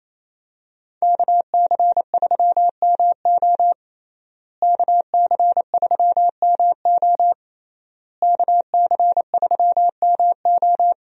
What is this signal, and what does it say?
Morse code, heard as a keyed tone, about 28 wpm, 700 Hz. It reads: KC3MO KC3MO KC3MO
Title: USA Call Signs - 5 Characters - Repeated 3x - 28wpm